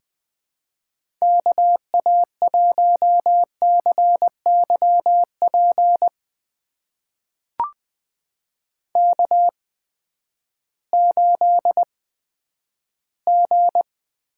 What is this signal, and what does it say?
Morse code, heard as a keyed tone, about 20 wpm, spaced as for 11 wpm, 700 Hz. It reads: KA1CYP K8G
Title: USA Call Signs - 8x Character Spacing - 20wpm